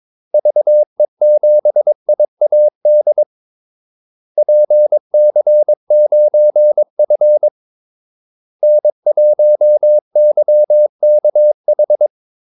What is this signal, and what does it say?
Morse code, heard as a keyed tone, about 22 wpm, 600 Hz. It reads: VE7IAD PC9F N1YKH
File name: Callsigns-22wpm